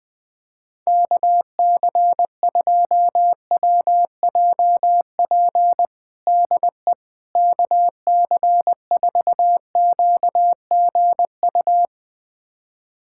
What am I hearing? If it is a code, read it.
Morse code, heard as a keyed tone, about 20 wpm, 700 Hz. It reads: KC2WJP DE KC4QGU
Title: USA Callsign DE Callsign - 6 Characters - 20wpm